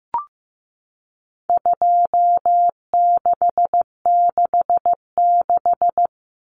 Morse code, heard as a keyed tone, about 15 wpm, 700 Hz. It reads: 2666
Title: Common Cut Numbers - 4-Digit 15wpm